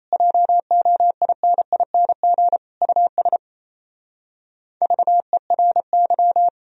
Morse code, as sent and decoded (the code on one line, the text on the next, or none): .--- --- .. -. .. -. --. / ..- ... / ...- . .-. -.--
JOINING US VERY